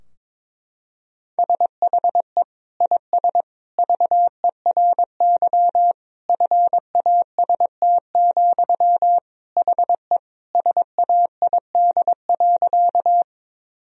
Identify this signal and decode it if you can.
Morse code, heard as a keyed tone, about 23 wpm, 700 Hz. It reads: SHE IS VERY FAST, HE SAID.
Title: Sentences from Top 1000 Words - Review Part 2 - 22wpm